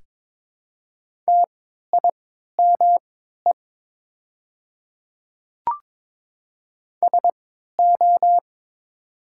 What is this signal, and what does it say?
Morse code, heard as a keyed tone, about 22 wpm, spaced as for 13 wpm, 700 Hz. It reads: TIME SO
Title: 3x Character Spacing - Top 100 Words - 22wpm